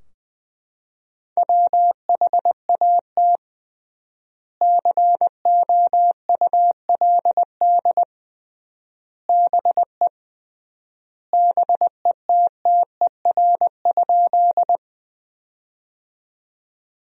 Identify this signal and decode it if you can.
Morse code, heard as a keyed tone, about 21 wpm, 700 Hz. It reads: WHAT COULD BE BETTER?
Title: 3x Word-Spacing - Sentences from Top 400 Words - 20wpm